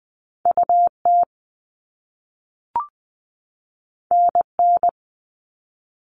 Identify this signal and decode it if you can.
Morse code, heard as a keyed tone, about 20 wpm, 700 Hz. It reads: UT NN
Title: All Cut Numbers - 2-Digit 20wpm